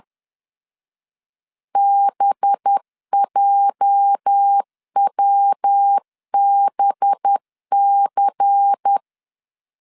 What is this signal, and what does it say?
Morse code, heard as a keyed tone, about 10 wpm, 800 Hz. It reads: BJWBC
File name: CW / DET / 1 9 1500C S A 10wpm 4511